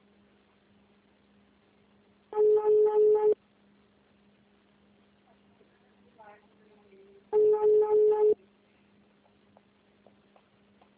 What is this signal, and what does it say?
Ringback tone (Europe, 425 Hz), with speech.
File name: out- -2053- - - .25001